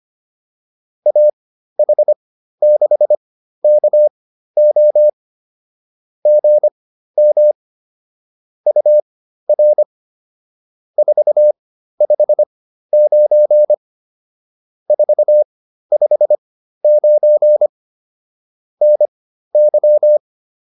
Morse code, heard as a keyed tone, about 25 wpm, spaced as for 13 wpm, 600 Hz. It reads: AH6KO GM UR 459 459 NY